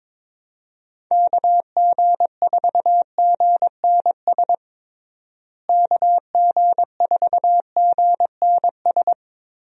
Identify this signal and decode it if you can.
Morse code, heard as a keyed tone, about 22 wpm, 700 Hz. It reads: KG4GNS KG4GNS
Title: USA Call Signs - 6 Characters - Repeated 2x - 22wpm